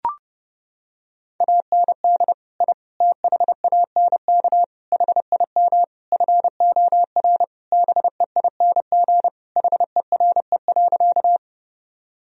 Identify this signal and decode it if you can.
Morse code, heard as a keyed tone, about 32 wpm, 700 Hz. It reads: AND I THANK HIM FOR BEING HERE.
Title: Sentences from Top 1000 Words - Review Part 3 - 30wpm